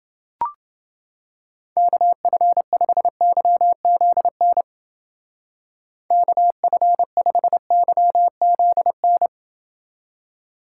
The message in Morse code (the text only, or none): KF5YZN KF5YZN